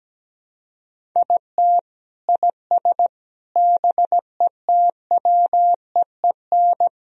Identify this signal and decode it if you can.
Morse code, heard as a keyed tone, about 17 wpm, 700 Hz. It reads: IT IS BETWEEN
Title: Sentences from Top 1000 Words - Review Part 1 - 17wpm